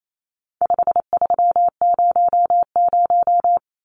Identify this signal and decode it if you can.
Morse code, heard as a keyed tone, about 28 wpm, 700 Hz. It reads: 5300
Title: Numbers - 4-Digit 28wpm